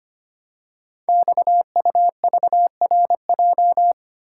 Morse code, heard as a keyed tone, about 25 wpm, 700 Hz. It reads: XUVRJ
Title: Code Groups - 5 Letters - 25wpm